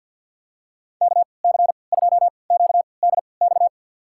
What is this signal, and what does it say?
Morse code, heard as a keyed tone, about 52 wpm, spaced as for 28 wpm, 700 Hz. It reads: KC2QDX
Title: USA Call Signs - 6 Characters - 3x Character Spacing - 50wpm